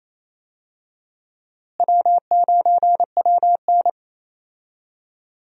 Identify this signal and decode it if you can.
Morse code, heard as a keyed tone, about 28 wpm, 700 Hz. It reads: W9WN
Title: USA Call Signs - 4 Characters - Repeated 3x - 28wpm